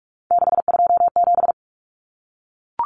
Morse code, heard as a keyed tone, about 44 wpm, 700 Hz. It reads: 627
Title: Numbers - 3-Digit 45wpm